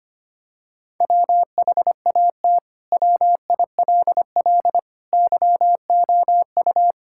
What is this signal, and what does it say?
Morse code, heard as a keyed tone, about 25 wpm, 700 Hz. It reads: WHAT WILL YOU
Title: Sentences from Top 1000 Words - Review Part 6 - 25wpm